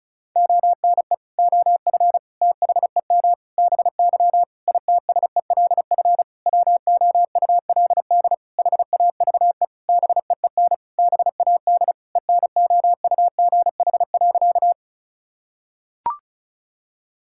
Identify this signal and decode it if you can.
Morse code, heard as a keyed tone, about 36 wpm, 700 Hz. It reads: ONE OF THEM BY ITSELF WOULD HAVE BEEN BAD ENOUGH.
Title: Sentences from Top 1000 Words - Review Part 5 - 35wpm